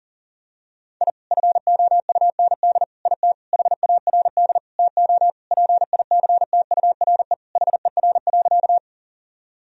Morse code, heard as a keyed tone, about 40 wpm, 700 Hz. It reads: I FOUND IT HARD TO PICTURE HER.
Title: Sentences from Top 1000 Words - Review Part 1 - 40wpm